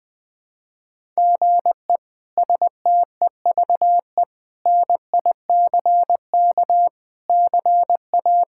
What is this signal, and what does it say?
Morse code, heard as a keyed tone, about 20 wpm, 700 Hz. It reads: GE STEVE NICK CA
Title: K1USN SST QSO Practice - 20wpm